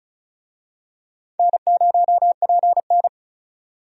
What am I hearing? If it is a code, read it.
Morse code, heard as a keyed tone, about 34 wpm, 700 Hz. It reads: N0PN
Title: USA Call Signs - 4 Characters - 35wpm